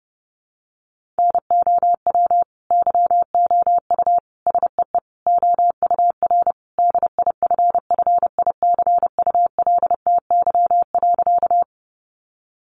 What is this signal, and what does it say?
Morse code, heard as a keyed tone, about 30 wpm, 700 Hz. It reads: NOW YOU SEE OUR DIFFICULTY.
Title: Sentences from Top 1500 Words 30wpm